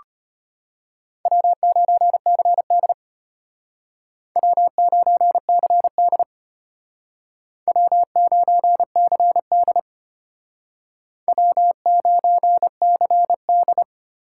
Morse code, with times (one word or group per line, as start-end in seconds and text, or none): 1.3-2.9 W9CD
4.4-6.2 W9CD
7.7-9.8 W9CD
11.3-13.8 W9CD